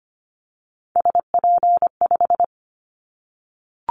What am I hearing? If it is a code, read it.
Morse code, heard as a keyed tone, about 25 wpm, 700 Hz. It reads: SP5
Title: Callsign Prefixes for Top 25 DXCC - 25wpm